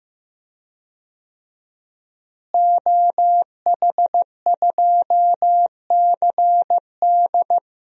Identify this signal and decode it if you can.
Morse code, heard as a keyed tone, about 16 wpm, 700 Hz. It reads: OH2CD
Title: DXCC-100 Call Signs - 5 Characters - Repeated 2x - 15wpm